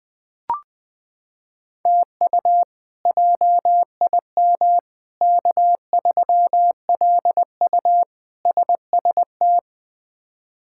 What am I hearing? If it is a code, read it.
Morse code, heard as a keyed tone, about 21 wpm, 700 Hz. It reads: TU JIM K3LU SST
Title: K1USN SST QSO Practice - 20wpm